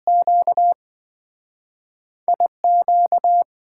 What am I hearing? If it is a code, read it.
Morse code, heard as a keyed tone, about 27 wpm, 700 Hz. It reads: Q IQ